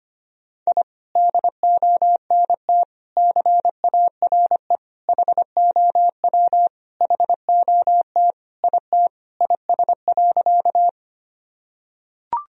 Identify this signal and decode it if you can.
Morse code, heard as a keyed tone, about 25 wpm, 700 Hz. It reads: I DONT CARE HOW HOT IT IS.
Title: Sentences from Top 1000 Words - Review Part 3 - 25wpm